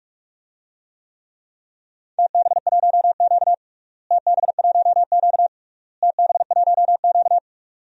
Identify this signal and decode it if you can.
Morse code, heard as a keyed tone, about 44 wpm, 700 Hz. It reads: TB1Q TB1Q TB1Q
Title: DXCC-100 Call Signs - 4 Characters - Repeated 3x - 45wpm